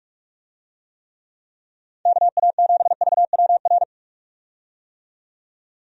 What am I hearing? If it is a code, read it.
Morse code, heard as a keyed tone, about 42 wpm, 700 Hz. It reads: KA7VWR
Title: USA Call Signs - 6 Characters - 45wpm